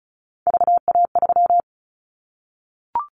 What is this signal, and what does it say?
Morse code, heard as a keyed tone, about 34 wpm, 700 Hz. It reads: VA3
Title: Callsign Prefixes for Top 50 DXCC - 35wpm